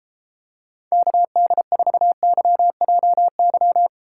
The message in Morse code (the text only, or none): KD4YJY